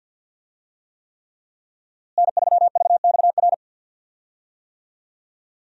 Morse code, heard as a keyed tone, about 52 wpm, 700 Hz. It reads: N3VXR
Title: USA Call Signs - 5 Characters - Repeated 3x - 50wpm